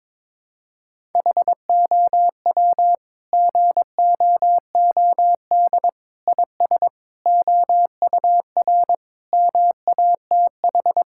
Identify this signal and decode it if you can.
Morse code, heard as a keyed tone, about 22 wpm, 700 Hz. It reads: HOW GOOD IS OUR MATH
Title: Sentences from Top 1000 Words - Review Part 6 - 22wpm